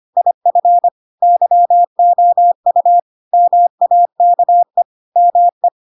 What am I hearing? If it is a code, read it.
Morse code, heard as a keyed tone, about 25 wpm, 700 Hz. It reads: IF YOU MAKE ME
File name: Pinocchio / Pinocchio ch4 25 25 ik1hge